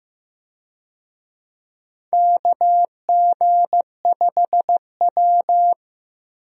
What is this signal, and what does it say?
Morse code, heard as a keyed tone, about 15 wpm, 700 Hz. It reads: KG5W
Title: USA Call Signs - 4 Characters - Repeated 2x - 15wpm